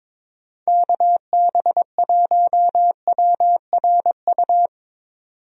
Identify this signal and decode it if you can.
Morse code, heard as a keyed tone, about 22 wpm, 700 Hz. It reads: KB1WRU
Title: USA Call Signs - 6 Characters - 22wpm